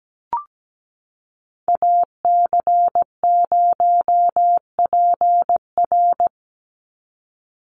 Morse code, heard as a keyed tone, about 17 wpm, 700 Hz. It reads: AC0PR
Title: NA SOTA Activator Callsigns 17wpm